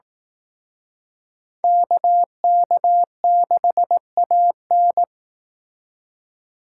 Morse code, heard as a keyed tone, about 19 wpm, 700 Hz. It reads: KK6AN
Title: Speed Racing - USA 2x2 Call Signs 15wpm